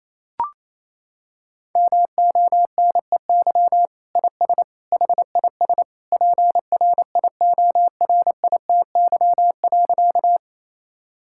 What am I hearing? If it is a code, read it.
Morse code, heard as a keyed tone, about 28 wpm, 700 Hz. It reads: MONEY IS HIS PRIORITY.
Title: Sentences from Top 1600 Words 28wpm